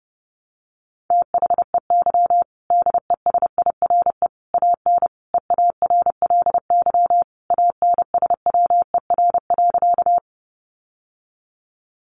Morse code, heard as a keyed tone, about 30 wpm, 700 Hz. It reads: THEY DESIRE AN EARLY ANSWER.
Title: Sentences from Top 1100 Words 30wpm